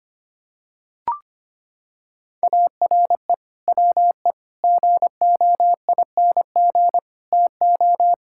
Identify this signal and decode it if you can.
Morse code, heard as a keyed tone, about 25 wpm, 700 Hz. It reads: ARE WE GOING TO
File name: practice / 25wpm / Sentences with 4-Word Phrases - 25wpm